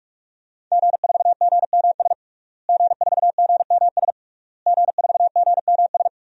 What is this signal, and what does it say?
Morse code, heard as a keyed tone, about 41 wpm, 700 Hz. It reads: G4GMS G4GMS G4GMS
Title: DXCC-100 Call Signs - 5 Characters - Repeated 3x - 45wpm